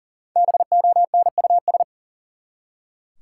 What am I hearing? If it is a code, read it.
Morse code, heard as a keyed tone, about 40 wpm, 700 Hz. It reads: BONUS
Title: Five Letter Words - 40wpm